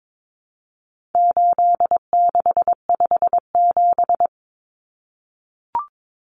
Morse code, heard as a keyed tone, about 22 wpm, 700 Hz. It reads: 8657
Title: Common Cut Numbers - 4-Digit 22wpm